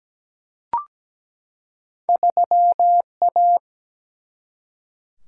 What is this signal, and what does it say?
Morse code, heard as a keyed tone, about 17 wpm, 700 Hz. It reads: 3A
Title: Single-Letter Single-Number 17wpm